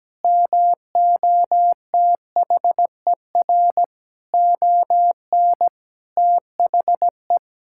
Morse code, heard as a keyed tone, about 17 wpm, 700 Hz. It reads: MOTHER ON THE
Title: Sentences from Top 1000 Words - Review Part 6 - 17wpm